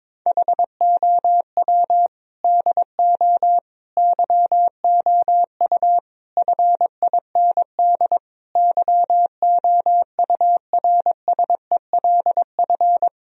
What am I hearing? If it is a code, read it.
Morse code, heard as a keyed tone, about 21 wpm, 700 Hz. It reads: HOW DO YOU FIND YOURSELF
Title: Sentences from Top 1000 Words - Review Part 4 - 22wpm